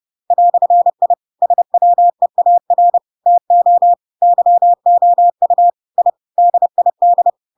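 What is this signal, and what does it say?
Morse code, heard as a keyed tone, about 30 wpm, 700 Hz. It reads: "I SWEAR TO YOU I DID
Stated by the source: Chapter 2 of The Adventures of Pinocchio read as a book in Morse code (CW) at 30 wpm, by ik1hge.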